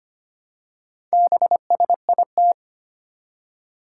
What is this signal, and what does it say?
Morse code, heard as a keyed tone, about 25 wpm, 700 Hz. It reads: BSIT
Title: Four Challenging Letter-Number Combinations 25wpm